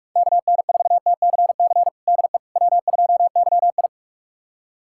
Morse code, heard as a keyed tone, about 46 wpm, 700 Hz. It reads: KN4TCC DE W2YI
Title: USA Callsign DE Callsign 45wpm